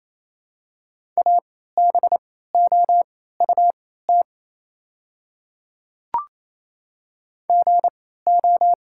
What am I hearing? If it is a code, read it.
Morse code, heard as a keyed tone, about 28 wpm, spaced as for 16 wpm, 700 Hz. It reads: ABOUT GO
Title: 3x Character Spacing - Top 100 Words - 28wpm